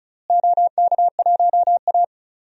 Morse code, heard as a keyed tone, about 36 wpm, 700 Hz. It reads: OK1A
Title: DXCC-100 Call Signs - 4 Characters - Repeated 2x - 35wpm